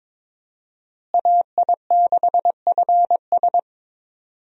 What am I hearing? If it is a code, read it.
Morse code, heard as a keyed tone, about 22 wpm, 700 Hz. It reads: AI6FS
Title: USA Call Signs - 5 Characters - Repeated 3x - 22wpm